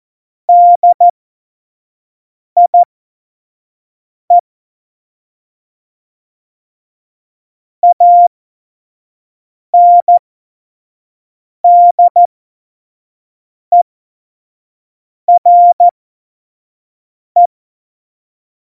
Morse code, heard as a keyed tone, about 14 wpm, spaced as for 5 wpm, 700 Hz. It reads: DIE ANDERE
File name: Morsekurs / Witze / Witz 03.Juli 2018, 25 F70